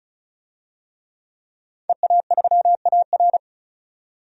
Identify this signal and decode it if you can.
Morse code, heard as a keyed tone, about 36 wpm, 700 Hz. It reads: EA3AR
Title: DXCC-100 Call Signs - 5 Characters - 35wpm